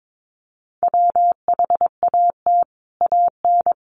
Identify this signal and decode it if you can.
Morse code, heard as a keyed tone, about 22 wpm, 700 Hz. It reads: WHAT AN
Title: Sentences from Top 1200 Words 22wpm